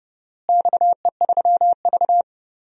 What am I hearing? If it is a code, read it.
Morse code, heard as a keyed tone, about 32 wpm, 700 Hz. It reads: XE3V
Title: DXCC-100 Call Signs - 4 Characters - Repeated 2x - 30wpm